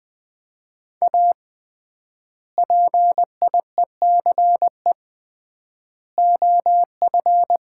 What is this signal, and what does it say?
Morse code, heard as a keyed tone, about 20 wpm, 700 Hz. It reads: A PIECE OF
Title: 3x Word-Spacing - Sets of 3 Words - 20wpm